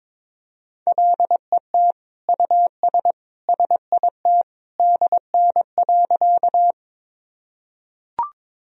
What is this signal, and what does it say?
Morse code, heard as a keyed tone, about 22 wpm, 700 Hz. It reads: LET US SIT DN.
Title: Sentences from Top 500 Words and Abbreviations 22wpm